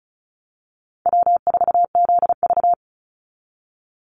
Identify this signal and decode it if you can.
Morse code, heard as a keyed tone, about 36 wpm, 700 Hz. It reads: W4ZV
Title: NA SOTA Chaser Callsigns 35wpm